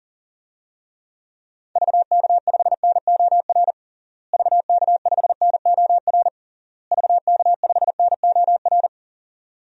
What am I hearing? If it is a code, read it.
Morse code, heard as a keyed tone, about 37 wpm, 700 Hz. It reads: VK5NOR VK5NOR VK5NOR
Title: DXCC-100 Call Signs - 6 Characters - Repeated 3x - 40wpm